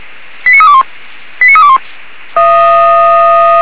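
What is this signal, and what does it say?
Five-tone selective call (ZVEI), heard as quick runs of five tones, tones of about 70 ms. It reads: 89321 89321
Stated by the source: Five-tone ZVEI selective call received by SDR, from audio.f85360.feuerwehr-orlamuende.org fire brigade server.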